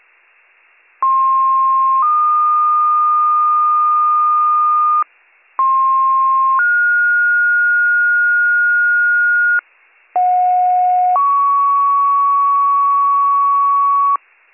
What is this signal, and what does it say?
Two-tone page, heard as a one-second tone, then a three-second tone, about 1060 then 1220 Hz, about 1060 then 1430 Hz, about 730 then 1120 Hz. It is heard from 1.0 to 14.2 s.